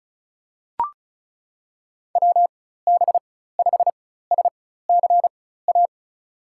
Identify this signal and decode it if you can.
Morse code, heard as a keyed tone, about 36 wpm, spaced as for 16 wpm, 700 Hz. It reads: WB5SCA
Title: USA Call Signs - 4x Character Spacing - 35wpm